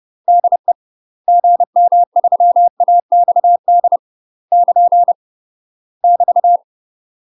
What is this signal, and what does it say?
Morse code, heard as a keyed tone, about 30 wpm, 700 Hz. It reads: DE GM3AXD <KN> <BT>
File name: UK MASTER QSOs TRAINING / 30WPM / UK MASTER QSOs 0019